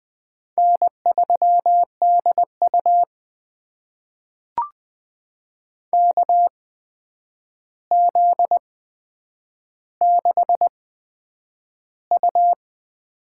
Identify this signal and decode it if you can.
Morse code, heard as a keyed tone, about 20 wpm, spaced as for 9 wpm, 700 Hz. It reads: N3DU KZ6U
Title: USA Call Signs - 4 Characters - 8x Character Spacing - 20wpm